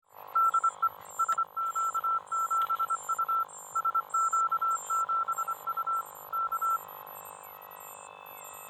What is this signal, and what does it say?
Morse code, heard as a keyed tone, about 24 wpm, 1340 Hz. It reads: DE SQ7U SQ7U K
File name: lampa-homodyn-cw